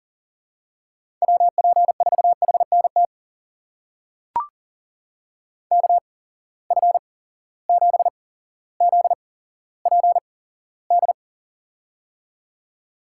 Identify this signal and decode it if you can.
Morse code, heard as a keyed tone, about 40 wpm, spaced as for 17 wpm, 700 Hz. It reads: WP4HNT KF7ZPD
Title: USA Call Signs - 6 Characters - 8x Character Spacing - 40wpm